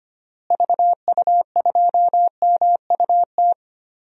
Morse code, heard as a keyed tone, about 25 wpm, 700 Hz. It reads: VU2MUT
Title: DXCC-100 Call Signs - 6 Characters - Repeated 2x - 25wpm